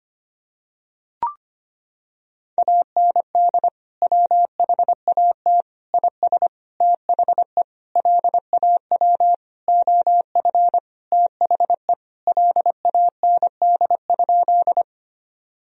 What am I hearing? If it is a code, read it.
Morse code, heard as a keyed tone, about 25 wpm, 700 Hz. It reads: AND WHAT IS THE LAW OF THE LAND?
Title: Sentences from Top 1000 Words - Review Part 3 - 25wpm